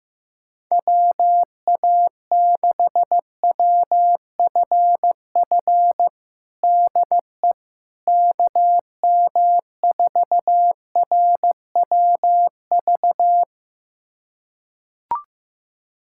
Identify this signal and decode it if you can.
Morse code, heard as a keyed tone, about 15 wpm, 700 Hz. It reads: WA6WFF DE KM4RWV
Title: USA Callsign DE Callsign 15wpm